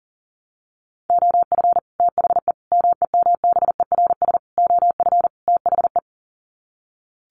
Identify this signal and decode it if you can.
Morse code, heard as a keyed tone, about 40 wpm, 700 Hz. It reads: OF THE MEMBERS OF THE
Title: Sets of 5 Words Encore2 40wpm